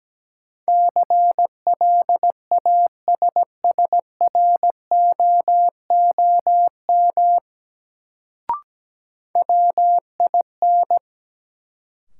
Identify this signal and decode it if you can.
Morse code, heard as a keyed tone, about 17 wpm, 700 Hz. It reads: CLASSROOM WIN
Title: Sentences from Top 1500 Words - New Word Drill - 17wpm